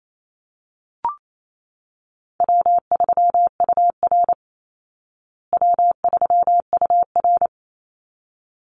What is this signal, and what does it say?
Morse code, heard as a keyed tone, about 28 wpm, 700 Hz. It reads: W3UR W3UR
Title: Call Signs Repeated 3x - 28wpm